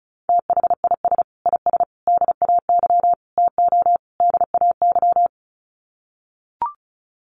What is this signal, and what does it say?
Morse code, heard as a keyed tone, about 34 wpm, 700 Hz. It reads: THIS IS DAY TO DAY
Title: Sets of 5 Words Encore 35wpm